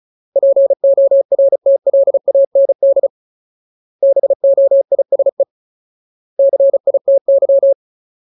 Morse code, heard as a keyed tone, about 34 wpm, 550 Hz. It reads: PORTLAND BOISE CITY
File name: Ultimate Word List / 35WPM Ultimate